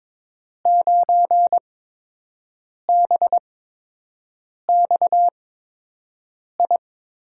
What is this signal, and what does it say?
Morse code, heard as a keyed tone, about 22 wpm, spaced as for 6 wpm, 700 Hz. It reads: 9BXI
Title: USA Call Signs - 6 Characters - 8x Character Spacing - 22wpm